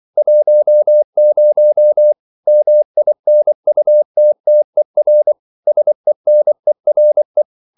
Morse code, heard as a keyed tone, about 24 wpm, 600 Hz. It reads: 10 MINUTTER SENERE
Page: Ferie 24 wpm